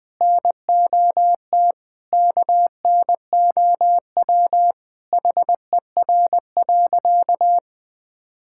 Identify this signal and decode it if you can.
Morse code, heard as a keyed tone, about 21 wpm, 700 Hz. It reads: NOT KNOW HER.
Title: Sentences with 4-Word Phrases - 20wpm